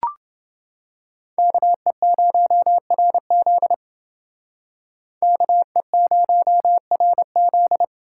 Morse code, heard as a keyed tone, about 28 wpm, 700 Hz. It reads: KE0RZ KE0RZ